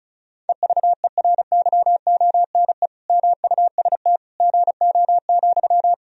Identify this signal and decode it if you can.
Morse code, heard as a keyed tone, about 36 wpm, 700 Hz. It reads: EVERYONE MUST GO,
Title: Sentences from Top 1000 Words - Review Part 4 - 35wpm